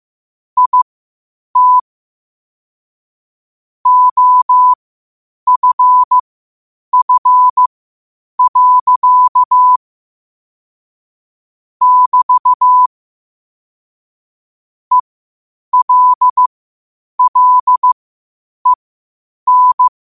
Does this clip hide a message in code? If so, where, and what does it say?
Yes: Morse code from 0.6 to 19.9 s, heard as a keyed tone, about 15 wpm, spaced as for 8 wpm, 1000 Hz. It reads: IT OFF. <BT> ELLEN